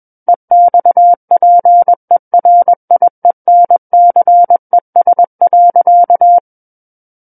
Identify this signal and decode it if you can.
Morse code, heard as a keyed tone, about 21 wpm, 700 Hz. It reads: EXPERIENCES.
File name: Book One / 12-Chapter 12 - 21 WPM